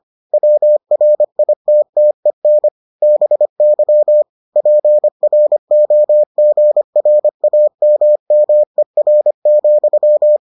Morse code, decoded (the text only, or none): WRITTEN BY PROGRAMMER,